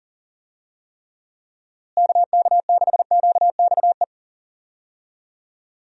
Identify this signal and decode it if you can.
Morse code, heard as a keyed tone, about 40 wpm, 700 Hz. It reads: KK6QXE
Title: USA Call Signs - 6 Characters - Repeated 2x - 40wpm